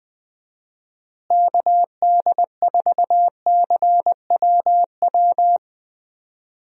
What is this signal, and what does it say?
Morse code, heard as a keyed tone, about 20 wpm, 700 Hz. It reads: KD4CWW
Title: USA Call Signs - 6 Characters - Repeated 3x - 20wpm